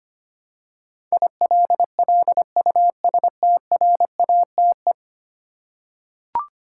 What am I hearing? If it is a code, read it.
Morse code, heard as a keyed tone, about 25 wpm, 700 Hz. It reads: ILLUSTRATE
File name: 10 Letter Words - First 3 Letters Given - 25wpm